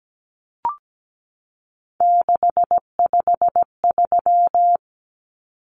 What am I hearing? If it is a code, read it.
Morse code, heard as a keyed tone, about 17 wpm, 700 Hz. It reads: 653
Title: Numbers - 3-Digit 17wpm